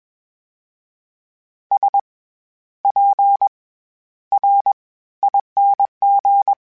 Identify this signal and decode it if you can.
Morse code, heard as a keyed tone, about 21 wpm, 800 Hz. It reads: S P R ING
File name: Special INGs-mp3s / Special INGs-mp3s / SlowToFaster ing @20